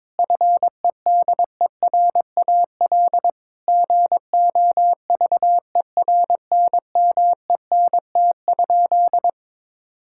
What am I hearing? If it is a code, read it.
Morse code, heard as a keyed tone, about 22 wpm, 700 Hz. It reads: FEDERAL GOVERNMENT?
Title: Sentences from Top 1000 Words - Review Part 5 - 22wpm